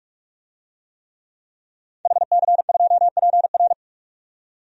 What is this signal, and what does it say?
Morse code, heard as a keyed tone, about 45 wpm, 700 Hz. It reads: HC2PR
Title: DXCC-100 Call Signs - 5 Characters - 45wpm